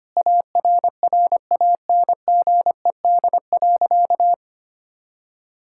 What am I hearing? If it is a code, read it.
Morse code, heard as a keyed tone, about 25 wpm, 700 Hz. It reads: ARRANGED.
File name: practice / 25wpm / Sentences with 3-Word Phrases - 25wpm